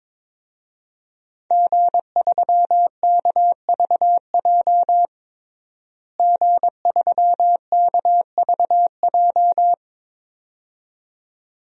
Morse code, heard as a keyed tone, about 21 wpm, 700 Hz. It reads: G3KVJ G3KVJ
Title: DXCC-100 Call Signs - 5 Characters - Repeated 2x - 22wpm